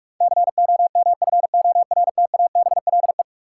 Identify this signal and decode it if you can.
Morse code, heard as a keyed tone, about 46 wpm, 700 Hz. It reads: COMFORTABLE
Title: Sentences from Top 1500 Words - New Word Drill Alt - 45wpm